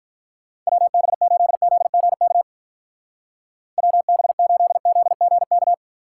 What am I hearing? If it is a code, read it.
Morse code, heard as a keyed tone, about 44 wpm, 700 Hz. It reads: WB8ZGK WB8ZGK